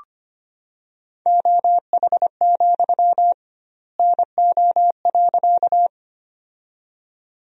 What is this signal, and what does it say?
Morse code, heard as a keyed tone, about 25 wpm, 700 Hz. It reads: OH, NO.
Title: 2x Word-Spacing - Sentences from Top 500 Words - 25wpm